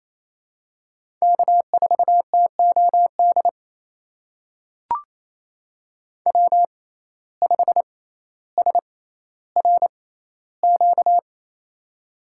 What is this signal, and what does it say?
Morse code, heard as a keyed tone, about 28 wpm, spaced as for 14 wpm, 700 Hz. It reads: K4TOD W5SRQ
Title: USA Call Signs - 5 Characters - 6x Character Spacing - 28wpm